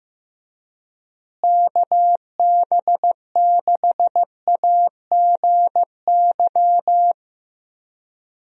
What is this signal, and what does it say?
Morse code, heard as a keyed tone, about 15 wpm, 700 Hz. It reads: KB6AGY
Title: USA Call Signs - 6 Characters - Repeated 3x - 15wpm